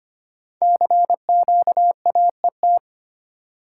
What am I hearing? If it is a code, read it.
Morse code, heard as a keyed tone, about 25 wpm, 700 Hz. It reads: CQAET
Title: Code Groups - 5 Letter:Numbers - 25wpm